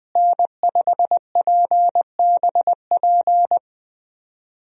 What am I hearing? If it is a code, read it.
Morse code, heard as a keyed tone, about 21 wpm, 700 Hz. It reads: N5PBP
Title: USA Call Signs - 5 Characters - Repeated 3x - 20wpm